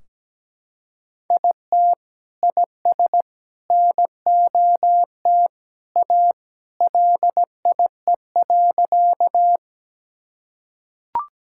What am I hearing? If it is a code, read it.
Morse code, heard as a keyed tone, about 17 wpm, 700 Hz. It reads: IT IS NOT A LIE.
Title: Sentences from Top 1600 Words 17wpm